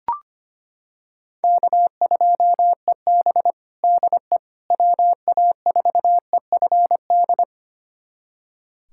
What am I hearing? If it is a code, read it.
Morse code, heard as a keyed tone, about 25 wpm, 700 Hz. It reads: K2EB DE WA4EFD
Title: USA Callsign DE Callsign 25wpm